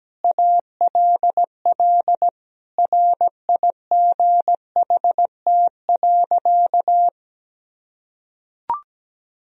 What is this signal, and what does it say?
Morse code, heard as a keyed tone, about 17 wpm, 700 Hz. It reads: ALL RIGHT.
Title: Sentences from Top 1000 Words - Review Part 1 - 17wpm